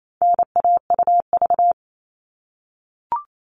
Morse code, heard as a keyed tone, about 28 wpm, 700 Hz. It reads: NAUV
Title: All Cut Numbers - 4-Digit 28wpm